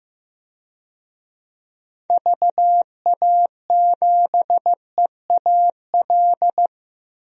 Morse code, heard as a keyed tone, about 15 wpm, 700 Hz. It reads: VA7EAL
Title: DXCC-100 Call Signs - 6 Characters - Repeated 3x - 15wpm